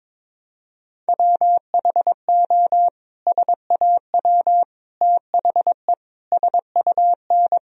Morse code, heard as a keyed tone, about 21 wpm, 700 Hz. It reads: WHO SAW THE SUN
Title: Sentences from Top 1000 Words - Review Part 1 - 22wpm